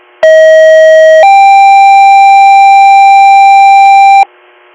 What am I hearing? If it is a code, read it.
Two-tone page, heard as a one-second tone, then a three-second tone, about 640 then 790 Hz, with noise. It is heard from 0.2 to 4.2 s.